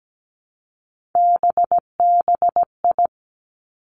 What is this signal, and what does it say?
Morse code, heard as a keyed tone, about 17 wpm, 700 Hz. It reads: BBI
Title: Three Challenging Letter-Number Combinations 17wpm